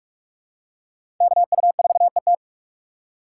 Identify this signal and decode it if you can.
Morse code, heard as a keyed tone, about 46 wpm, 700 Hz. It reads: KU4ET
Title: USA Call Signs - 5 Characters - 4x Character Spacing - 45wpm